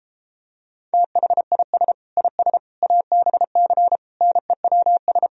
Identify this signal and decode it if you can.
Morse code, heard as a keyed tone, about 33 wpm, 700 Hz. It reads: THIS IS ABC NEWS